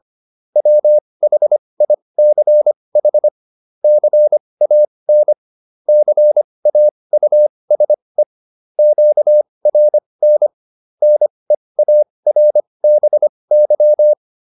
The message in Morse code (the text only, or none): WHICH CAN CAUSE QRN NEARBY